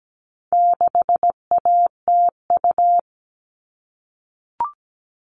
Morse code, heard as a keyed tone, about 17 wpm, 700 Hz. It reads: 6ATU
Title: All Cut Numbers - 4-Digit 17wpm